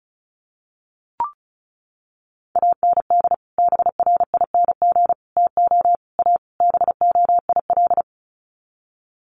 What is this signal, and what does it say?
Morse code, heard as a keyed tone, about 34 wpm, 700 Hz. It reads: AND BRING TO A BOIL
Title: Sets of 5 Words Encore 35wpm